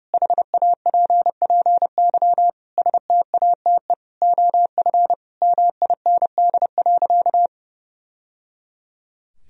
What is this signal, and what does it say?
Morse code, heard as a keyed tone, about 30 wpm, 700 Hz. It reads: HAPPY STATE OF MIND.
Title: Sentences from Top 1000 Words - Review Part 2 - 30wpm